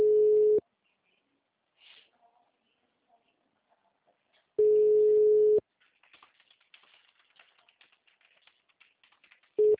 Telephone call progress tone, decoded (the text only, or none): ringback tone